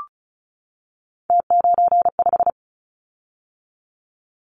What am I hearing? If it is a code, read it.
Morse code, heard as a keyed tone, about 36 wpm, 700 Hz. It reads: T95
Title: Callsign Prefixes for Top 100 DXCC - 35wpm